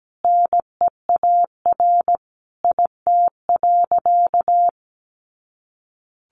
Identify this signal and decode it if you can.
Morse code, heard as a keyed tone, about 17 wpm, 700 Hz. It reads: NEAR IT.
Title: A Sound Of Thunder 17wpm